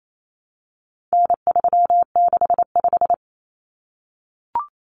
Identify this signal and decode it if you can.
Morse code, heard as a keyed tone, about 28 wpm, 700 Hz. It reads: N365
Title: Common Cut Numbers - 4-Digit 28wpm